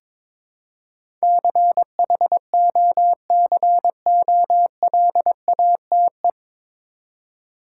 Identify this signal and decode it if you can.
Morse code, heard as a keyed tone, about 22 wpm, 700 Hz. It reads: CHOCOLATE
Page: Morse Code Ninja practice 22wpm